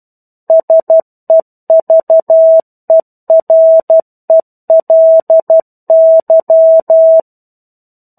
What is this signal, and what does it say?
Morse code, heard as a keyed tone, about 12 wpm, 650 Hz. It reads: SEVERELY
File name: 01-Chapter-01 12 WPM